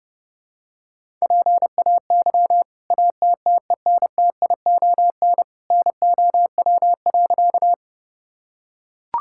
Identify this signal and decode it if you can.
Morse code, heard as a keyed tone, about 30 wpm, 700 Hz. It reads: PAY ATTENTION NOW.
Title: Sentences from Top 1000 Words - Review Part 6 - 30wpm